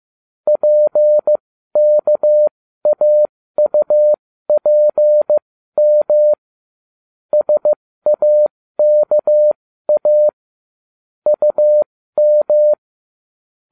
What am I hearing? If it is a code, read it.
Morse code, heard as a keyed tone, about 15 wpm, spaced as for 13 wpm, 600 Hz. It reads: PKAUPM SAKA UM